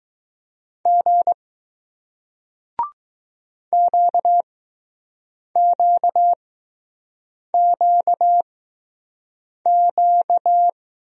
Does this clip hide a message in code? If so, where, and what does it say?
Yes: Morse code from 0.9 to 10.7 s, heard as a keyed tone, about 21 wpm, spaced as for 6 wpm, 700 Hz. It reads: G QQQQ